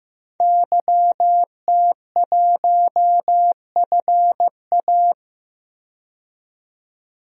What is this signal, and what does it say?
Morse code, heard as a keyed tone, about 15 wpm, 700 Hz. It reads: YT1FA
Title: DXCC-100 Call Signs - 5 Characters - 15wpm